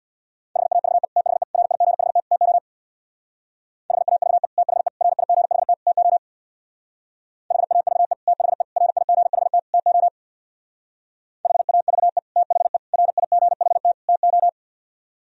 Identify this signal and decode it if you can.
Morse code, heard as a keyed tone, about 64 wpm, 700 Hz. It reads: HAVE THE RIGHT TO HAVE THE RIGHT TO HAVE THE RIGHT TO HAVE THE RIGHT TO